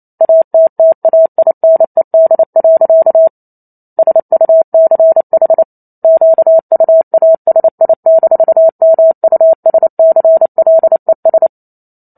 Morse code, heard as a keyed tone, about 28 wpm, 650 Hz. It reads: ATTAINED. SUCH QUASI-MUSCLES
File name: Chapter 2 - 28 WPM